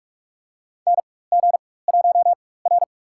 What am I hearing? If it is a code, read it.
Morse code, heard as a keyed tone, about 43 wpm, spaced as for 21 wpm, 700 Hz. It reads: NG1R
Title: USA Call Signs - 4 Characters - 4x Character Spacing - 45wpm